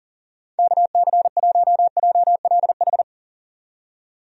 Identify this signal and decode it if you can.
Morse code, heard as a keyed tone, about 40 wpm, 700 Hz. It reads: KC1JLH
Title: USA Call Signs - 6 Characters - 40wpm